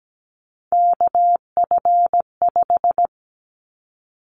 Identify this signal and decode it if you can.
Morse code, heard as a keyed tone, about 17 wpm, 700 Hz. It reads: KF5
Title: Callsign Prefixes for Top 25 DXCC - Repeated 3x - 17wpm